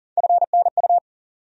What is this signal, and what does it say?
Morse code, heard as a keyed tone, about 40 wpm, 700 Hz. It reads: FNU
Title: USA Call Sign Postfixes - 3 Letters 40wpm